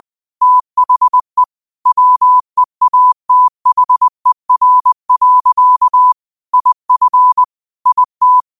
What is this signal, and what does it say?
Morse code, heard as a keyed tone, about 20 wpm, 1000 Hz. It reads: THE WEATHER. IF IT
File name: qotdfiles / 20wpm